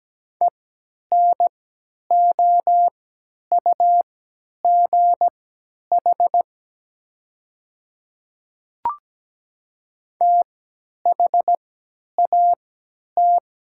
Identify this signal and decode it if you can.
Morse code, heard as a keyed tone, about 17 wpm, spaced as for 10 wpm, 700 Hz. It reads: ENOUGH THAT
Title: 3x Character Spacing - Top 500 Words - 17wpm